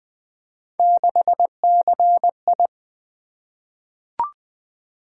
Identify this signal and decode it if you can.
Morse code, heard as a keyed tone, about 21 wpm, 700 Hz. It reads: 6CI
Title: Three Letter-Number Combinations - 20wpm